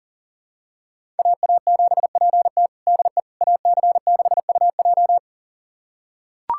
Morse code, heard as a keyed tone, about 40 wpm, 700 Hz. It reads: AA7PT DE AC6UJ
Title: USA Callsign DE Callsign - 5 Characters - 40wpm